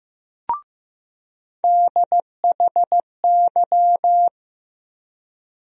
Morse code, heard as a keyed tone, about 15 wpm, 700 Hz. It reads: DHY
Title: USA Call Sign Postfixes - 3 Letters 15wpm